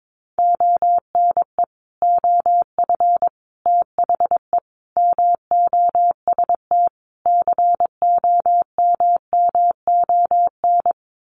Morse code, heard as a keyed tone, about 22 wpm, 700 Hz. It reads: ONE OF THE MOST COMMON
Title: Sets of 5 Words Encore2 22wpm